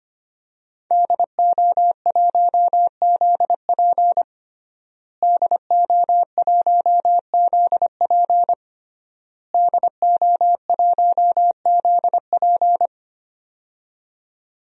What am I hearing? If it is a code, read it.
Morse code, heard as a keyed tone, about 25 wpm, 700 Hz. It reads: DO1ZP DO1ZP DO1ZP
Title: DXCC-100 Call Signs - 5 Characters - Repeated 3x - 25wpm